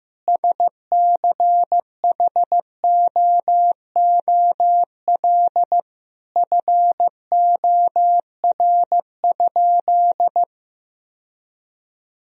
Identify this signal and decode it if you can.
Morse code, heard as a keyed tone, about 15 wpm, 700 Hz. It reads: SCHOOL FOR?
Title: Sentences from Top 1000 Words - Review Part 1 - 15wpm